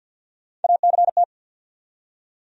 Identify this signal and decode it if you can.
Morse code, heard as a keyed tone, about 67 wpm, spaced as for 58 wpm, 700 Hz. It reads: ACT
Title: Speed-Racing Top 200 Words 50wpm